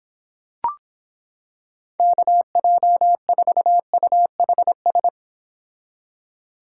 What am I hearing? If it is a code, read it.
Morse code, heard as a keyed tone, about 26 wpm, 700 Hz. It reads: KJ4UHS